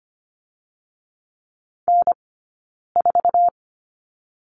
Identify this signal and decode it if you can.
Morse code, heard as a keyed tone, about 25 wpm, 700 Hz. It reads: N 4
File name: course / 25:10wpm / 092 Lesson 16d - Letter c - Two Characters 25wpm